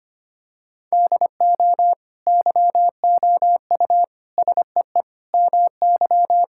Morse code, heard as a keyed tone, about 25 wpm, 700 Hz. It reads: DO YOU SEE MY
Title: Sentences from Top 1600 Words 25wpm